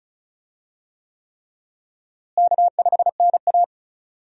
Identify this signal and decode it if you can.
Morse code, heard as a keyed tone, about 36 wpm, 700 Hz. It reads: K5NA
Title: USA Call Signs - 4 Characters - Repeated 3x - 35wpm